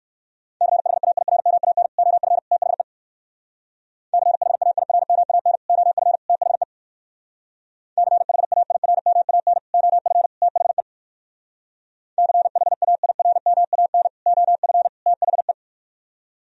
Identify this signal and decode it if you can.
Morse code, heard as a keyed tone, about 57 wpm, 700 Hz. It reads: CHAIRMAN OF THE CHAIRMAN OF THE CHAIRMAN OF THE CHAIRMAN OF THE